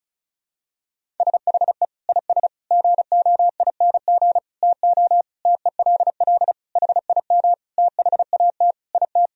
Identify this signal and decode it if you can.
Morse code, heard as a keyed tone, about 36 wpm, 700 Hz. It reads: SHE IS GOING TO TELL HIM THAT IT
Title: Sentences from Top 1000 Words - Review Part 1 - 35wpm